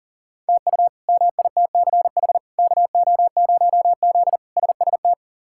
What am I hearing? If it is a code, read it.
Morse code, heard as a keyed tone, about 37 wpm, 700 Hz. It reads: TU MITCH KO0Z SST
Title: K1USN SST QSO Practice - 40wpm